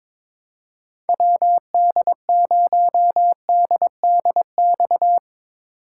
Morse code, heard as a keyed tone, about 22 wpm, 700 Hz. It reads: WD0DDX
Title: USA Call Signs - 6 Characters - 2x Character Spacing - 22wpm